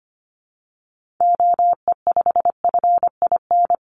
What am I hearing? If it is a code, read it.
Morse code, heard as a keyed tone, about 25 wpm, 700 Hz. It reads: OE5FIN
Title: Call Signs - 25wpm